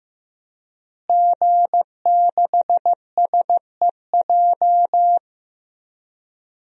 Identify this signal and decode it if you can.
Morse code, heard as a keyed tone, about 15 wpm, 700 Hz. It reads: G6SEJ
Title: DXCC-100 Call Signs - 5 Characters - Repeated 2x - 15wpm